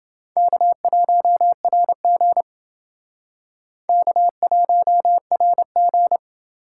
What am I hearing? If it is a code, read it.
Morse code, heard as a keyed tone, about 28 wpm, 700 Hz. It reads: K1RG K1RG